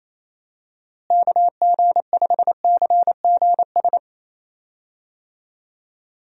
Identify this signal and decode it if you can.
Morse code, heard as a keyed tone, about 28 wpm, 700 Hz. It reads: KG5CGS
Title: USA Call Signs - 6 Characters - Repeated 3x - 28wpm